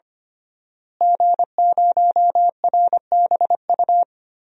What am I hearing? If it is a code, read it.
Morse code, heard as a keyed tone, about 25 wpm, 700 Hz. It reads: G0RBU
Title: DXCC-100 Call Signs - 5 Characters - Repeated 3x - 25wpm